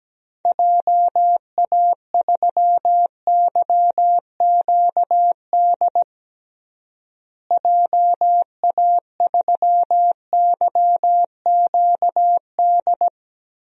Morse code, heard as a keyed tone, about 17 wpm, 700 Hz. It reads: JA3YQD JA3YQD
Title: DXCC-100 Call Signs - 6 Characters - Repeated 3x - 17wpm